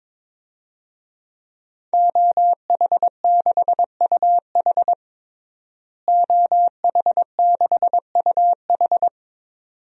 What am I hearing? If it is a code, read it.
Morse code, heard as a keyed tone, about 22 wpm, 700 Hz. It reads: OH6UH OH6UH
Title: DXCC-100 Call Signs - 5 Characters - Repeated 2x - 22wpm